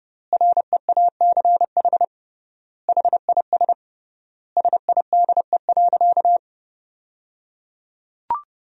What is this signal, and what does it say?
Morse code, heard as a keyed tone, about 29 wpm, 700 Hz. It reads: REACH HIS SIDE.
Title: 3x Word-Spacing - Sentences from Top 400 Words - 30wpm